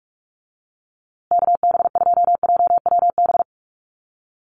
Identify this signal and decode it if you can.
Morse code, heard as a keyed tone, about 44 wpm, 700 Hz. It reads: KB2JWB
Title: USA Call Signs Repeated 3x 45wpm